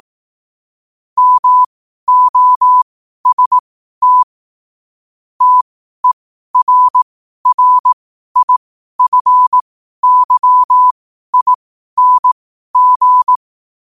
Morse code, heard as a keyed tone, about 18 wpm, spaced as for 12 wpm, 1000 Hz. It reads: MOST TERRIFYING
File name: qotdfiles / 12wpm